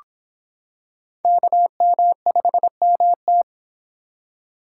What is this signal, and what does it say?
Morse code, heard as a keyed tone, about 25 wpm, 700 Hz. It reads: KM5MT